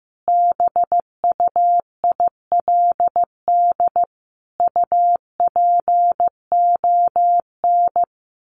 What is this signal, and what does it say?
Morse code, heard as a keyed tone, about 15 wpm, 700 Hz. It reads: BUILD UPON
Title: Sentences from Top 1400 Words 15wpm